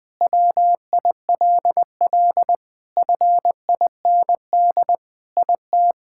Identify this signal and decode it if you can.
Morse code, heard as a keyed tone, about 20 wpm, 700 Hz. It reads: WILL FIND IT
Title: Sentences with 3-Word Phrases - 20wpm